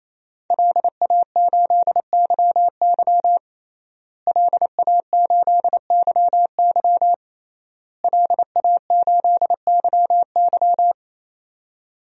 Morse code, heard as a keyed tone, about 28 wpm, 700 Hz. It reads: LA8YY LA8YY LA8YY
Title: DXCC-100 Call Signs - 5 Characters - Repeated 3x - 28wpm